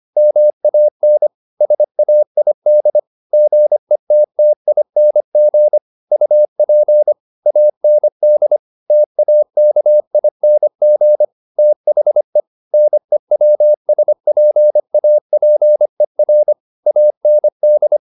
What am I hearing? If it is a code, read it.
Morse code, heard as a keyed tone, about 25 wpm, 600 Hz. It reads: MAN SAID GETTING UP AND TAKING THE NEWSPAPER AND
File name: the old man and the sea part 1 25wpm